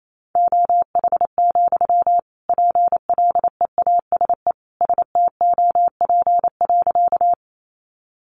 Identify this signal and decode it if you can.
Morse code, heard as a keyed tone, about 28 wpm, 700 Hz. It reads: OH, PLEASE STOP.
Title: Sentences from Top 1500 Words 28wpm